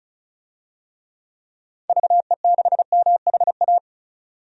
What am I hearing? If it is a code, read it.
Morse code, heard as a keyed tone, about 34 wpm, 700 Hz. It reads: VE6MHA
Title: DXCC-100 Call Signs - 6 Characters - 35wpm